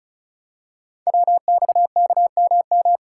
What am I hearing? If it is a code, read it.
Morse code, heard as a keyed tone, about 36 wpm, 700 Hz. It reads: WXKMM
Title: Code Groups - 5 Letters - 35wpm